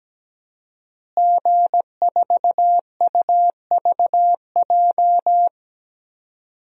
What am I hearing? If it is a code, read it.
Morse code, heard as a keyed tone, about 17 wpm, 700 Hz. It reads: G4UVJ
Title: DXCC-100 Call Signs - 5 Characters - Repeated 3x - 17wpm